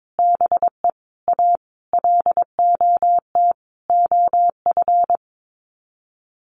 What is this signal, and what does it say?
Morse code, heard as a keyed tone, about 22 wpm, 700 Hz. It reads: BE A LOT OF
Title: Sets of 5 Words Encore 22wpm